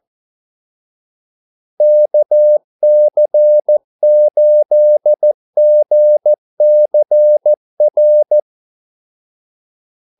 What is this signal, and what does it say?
Morse code, heard as a keyed tone, about 14 wpm, 600 Hz. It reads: KC8GCR
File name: Callsigns-14wpm